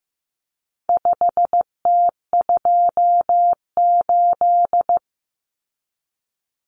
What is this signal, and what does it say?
Morse code, heard as a keyed tone, about 15 wpm, 700 Hz. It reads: 5T28
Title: Common Cut Numbers - 4-Digit 15wpm